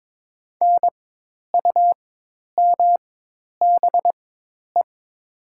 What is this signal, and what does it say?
Morse code, heard as a keyed tone, about 22 wpm, spaced as for 10 wpm, 700 Hz. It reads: NUMBE
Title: 4x Character Spacing - Top 100 Words - 22wpm